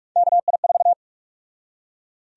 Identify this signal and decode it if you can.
Morse code, heard as a keyed tone, about 42 wpm, 700 Hz. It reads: KI4
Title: USA Call Sign Prefixes - 45wpm